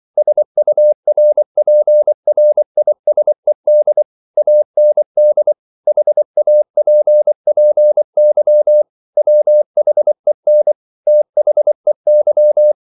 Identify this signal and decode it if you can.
Morse code, heard as a keyed tone, about 24 wpm, 600 Hz. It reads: SURPRISED AND HAPPY WHEN THEY